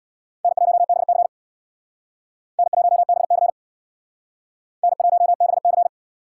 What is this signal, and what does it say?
Morse code, heard as a keyed tone, about 61 wpm, 700 Hz. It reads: N1BC N1BC N1BC
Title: Speed Racing - USA 1x2 Call Signs 50wpm